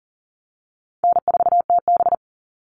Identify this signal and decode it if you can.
Morse code, heard as a keyed tone, about 40 wpm, 700 Hz. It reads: N4TB
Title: Easy Call Signs - 40wpm